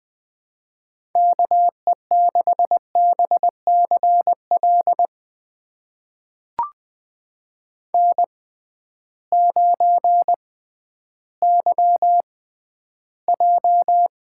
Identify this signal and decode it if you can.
Morse code, heard as a keyed tone, about 20 wpm, spaced as for 12 wpm, 700 Hz. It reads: KE6BCL N9YJ
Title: USA Call Signs - 6x Character Spacing - 20wpm